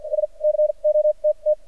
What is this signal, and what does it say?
Morse code, heard as a keyed tone, about 50 wpm, spaced as for 35 wpm, 610 Hz. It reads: SCOTT
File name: archived / CRSnet